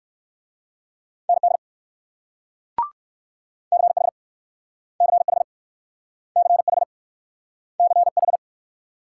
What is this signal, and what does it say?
Morse code, heard as a keyed tone, about 64 wpm, 700 Hz. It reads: ND CH CH CH CH